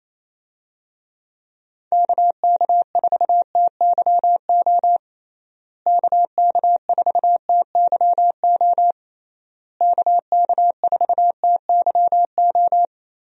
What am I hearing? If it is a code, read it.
Morse code, heard as a keyed tone, about 28 wpm, 700 Hz. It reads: KK4TYO KK4TYO KK4TYO
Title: USA Call Signs - 6 Characters - Repeated 3x - 28wpm